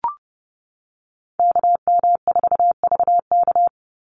Morse code, heard as a keyed tone, about 30 wpm, 700 Hz. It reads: KM4VK
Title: USA Call Signs 30wpm